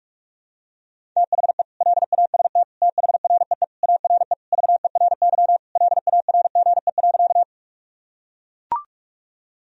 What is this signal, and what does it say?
Morse code, heard as a keyed tone, about 42 wpm, 700 Hz. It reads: THE LAST THREE ARE VERY LARGE.
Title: Sentences from Top 1000 Words - Review Part 1 - 45wpm